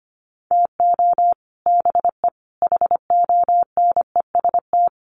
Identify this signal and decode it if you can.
Morse code, heard as a keyed tone, about 25 wpm, 700 Hz. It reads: TO BE HONEST
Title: Sets of 5 Words Encore 25wpm